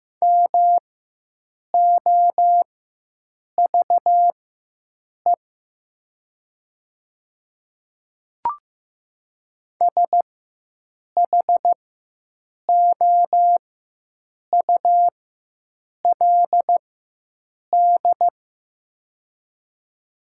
Morse code, heard as a keyed tone, about 15 wpm, spaced as for 7 wpm, 700 Hz. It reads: MOVE SHOULD
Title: 4x Character Spacing - Top 300 Words - 15wpm